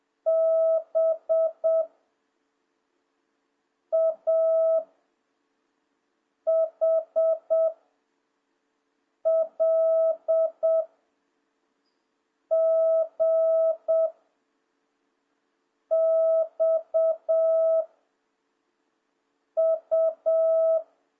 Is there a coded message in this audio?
Yes: Morse code from 0.3 to 20.8 s, heard as a keyed tone, about 7 wpm, 660 Hz. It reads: B A H L G X U